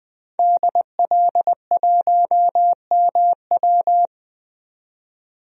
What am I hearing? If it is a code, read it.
Morse code, heard as a keyed tone, about 20 wpm, 700 Hz. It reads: DL1MW
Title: DXCC-100 Call Signs - 5 Characters - 20wpm